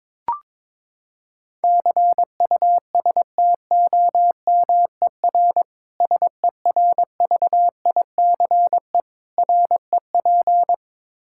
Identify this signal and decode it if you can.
Morse code, heard as a keyed tone, about 22 wpm, 700 Hz. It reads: CUSTOMER SERVICE REP
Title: Practice QSO - Exchange 3 - More Info - 22wpm